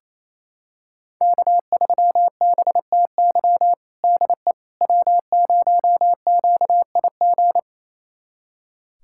Morse code, heard as a keyed tone, about 28 wpm, 700 Hz. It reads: K3BTY DE W0QIG
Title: USA Callsign DE Callsign - 5 Characters - 28wpm